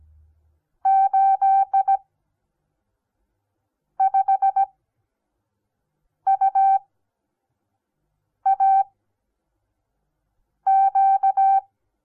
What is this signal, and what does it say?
Morse code, heard as a keyed tone, about 17 wpm, spaced as for 4 wpm, 780 Hz. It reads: 85UAQ
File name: Onlinelehrgang / morsen / DJ4UF Morsen08